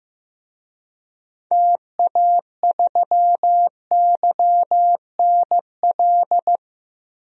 Morse code, heard as a keyed tone, about 15 wpm, 700 Hz. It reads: TA3YNL
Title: DXCC-100 Call Signs - 6 Characters - Repeated 2x - 15wpm